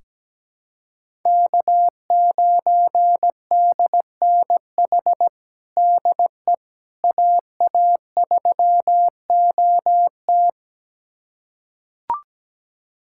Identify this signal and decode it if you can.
Morse code, heard as a keyed tone, about 17 wpm, 700 Hz. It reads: K9DNH DE AA3OT
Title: USA Callsign DE Callsign - 5 Characters - 17wpm